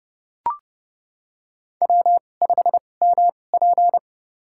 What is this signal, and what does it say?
Morse code, heard as a keyed tone, about 30 wpm, spaced as for 22 wpm, 700 Hz. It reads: W5MP
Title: USA Call Signs - 4 Characters - 2x Character Spacing - 30wpm